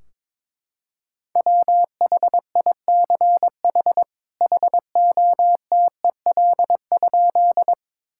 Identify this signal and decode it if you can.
Morse code, heard as a keyed tone, about 22 wpm, 700 Hz. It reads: WHICH HOTEL?
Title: Sentences from Top 1600 Words 22wpm